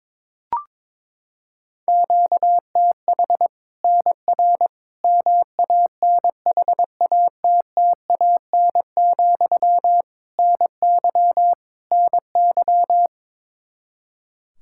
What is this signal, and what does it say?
Morse code, heard as a keyed tone, about 22 wpm, 700 Hz. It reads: QTH NR MANHATTAN, NY NY
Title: Practice QSO - Exchange 1 - Beginning - 22wpm